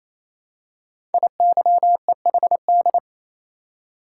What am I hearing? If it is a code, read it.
Morse code, heard as a keyed tone, about 27 wpm, 700 Hz. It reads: IYEHD
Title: Code Groups - 5 Letters - 28wpm